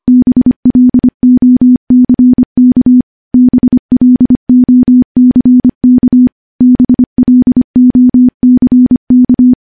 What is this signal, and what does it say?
Morse code, heard as a keyed tone, about 25 wpm, 250 Hz. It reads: BLOCK BLOCK BLOCK
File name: morse-training-25WPM